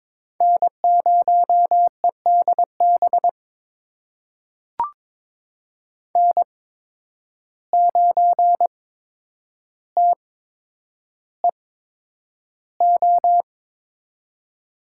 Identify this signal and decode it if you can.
Morse code, heard as a keyed tone, about 22 wpm, spaced as for 10 wpm, 700 Hz. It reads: N0EDB N9TEO
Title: USA Call Signs - 5 Characters - 8x Character Spacing - 22wpm